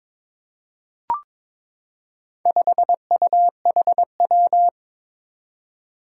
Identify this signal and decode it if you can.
Morse code, heard as a keyed tone, about 23 wpm, 700 Hz. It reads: 5UHW
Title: Four Challenging Letter-Number Combinations 22wpm